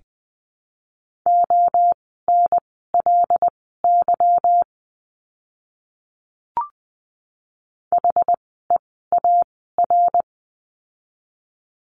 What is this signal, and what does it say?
Morse code, heard as a keyed tone, about 20 wpm, spaced as for 14 wpm, 700 Hz. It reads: ONLY HEAR
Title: 206 Lesson 33f - Letter j - Words 20wpm